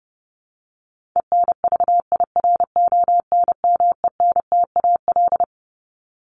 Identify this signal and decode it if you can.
Morse code, heard as a keyed tone, about 28 wpm, 700 Hz. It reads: ENVIRONMENTAL
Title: Top 1001-1250 Words 30wpm